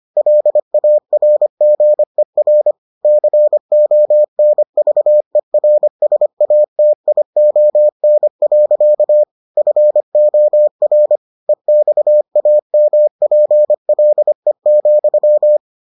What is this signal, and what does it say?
Morse code, heard as a keyed tone, about 25 wpm, 600 Hz. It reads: LARGER CONVERSATION. FOR EXAMPLE,